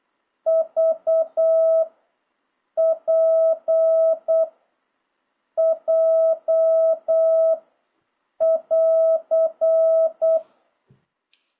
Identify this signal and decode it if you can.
Morse code, heard as a keyed tone, about 8 wpm, 650 Hz. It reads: V P J <AR>
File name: jh8cbh / 2026cw